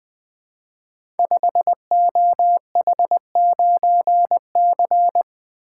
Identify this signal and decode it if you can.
Morse code, heard as a keyed tone, about 20 wpm, 700 Hz. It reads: 5OH9C
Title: Code Groups - 5 Letter:Numbers - 20wpm